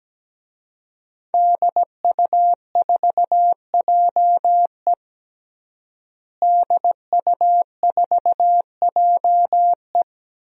Morse code, heard as a keyed tone, about 17 wpm, 700 Hz. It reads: DU4JE DU4JE
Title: DXCC-100 Call Signs - 5 Characters - Repeated 3x - 17wpm